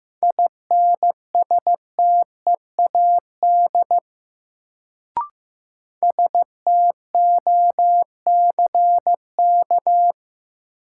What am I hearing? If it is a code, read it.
Morse code, heard as a keyed tone, about 15 wpm, 700 Hz. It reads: INSTEAD STOCK
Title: Sentences from Top 800 Words - New Word Drill - 15wpm